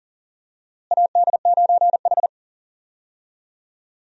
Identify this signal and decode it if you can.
Morse code, heard as a keyed tone, about 40 wpm, 700 Hz. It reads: AD9H
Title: USA Call Signs - 4 Characters - Repeated 3x - 40wpm